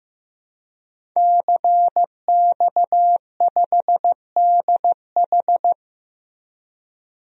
Morse code, heard as a keyed tone, about 16 wpm, 700 Hz. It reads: CX5DH
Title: DXCC-100 Call Signs - 5 Characters - 15wpm